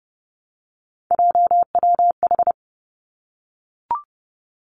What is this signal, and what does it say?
Morse code, heard as a keyed tone, about 30 wpm, 700 Hz. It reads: JWH
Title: Three Challenging Letter-Number Combinations 30wpm